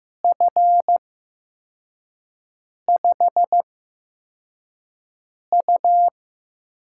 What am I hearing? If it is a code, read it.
Morse code, heard as a keyed tone, about 15 wpm, spaced as for 4 wpm, 700 Hz. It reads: F5U
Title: USA Call Signs - 6 Characters - 8x Character Spacing - 15wpm